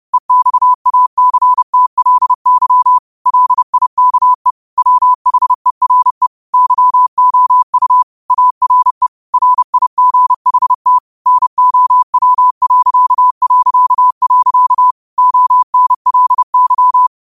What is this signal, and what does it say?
Morse code, heard as a keyed tone, about 30 wpm, 1000 Hz. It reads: EXACTLY LIKE WHERE YOU ARE RIGHT NOW... ONLY
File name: qotdfiles / 30wpm